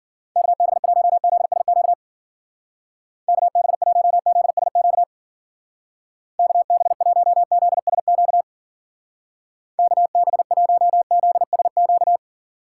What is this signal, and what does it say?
Morse code, heard as a keyed tone, about 52 wpm, 700 Hz. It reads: KB1ZSQ KB1ZSQ KB1ZSQ KB1ZSQ